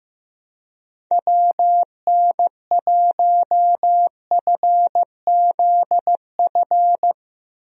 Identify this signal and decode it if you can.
Morse code, heard as a keyed tone, about 15 wpm, 700 Hz. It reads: WN1FZF
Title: USA Call Signs - 6 Characters - Repeated 2x - 15wpm